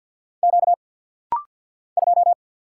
Morse code, heard as a keyed tone, about 50 wpm, 690 Hz. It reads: Q 2
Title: Single Letter-Number - Rapid-Fire 50wpm